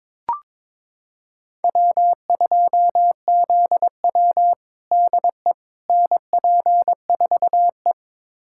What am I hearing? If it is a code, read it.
Morse code, heard as a keyed tone, about 22 wpm, 700 Hz. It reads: W2ZW DE NP4E
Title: USA Callsign DE Callsign - 4 Characters - 22wpm